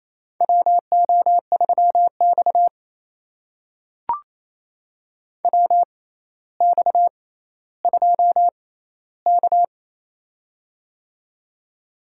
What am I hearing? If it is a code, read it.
Morse code, heard as a keyed tone, about 28 wpm, spaced as for 14 wpm, 700 Hz. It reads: WO3X WX2K
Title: USA Call Signs - 4 Characters - 6x Character Spacing - 28wpm